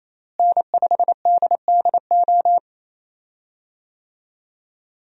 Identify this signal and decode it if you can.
Morse code, heard as a keyed tone, about 28 wpm, 700 Hz. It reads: N5DDO
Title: USA Call Signs - 5 Characters - Repeated 3x - 28wpm